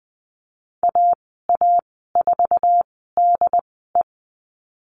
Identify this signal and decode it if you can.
Morse code, heard as a keyed tone, about 20 wpm, spaced as for 15 wpm, 700 Hz. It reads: AA4DE
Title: 076 Lesson 13g - Letter L - Call Signs 20wpm